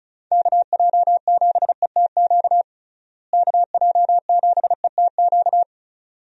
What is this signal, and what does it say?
Morse code, heard as a keyed tone, about 36 wpm, 700 Hz. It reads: KJ7ETQ KJ7ETQ
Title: USA Call Signs - 6 Characters - Repeated 2x - 35wpm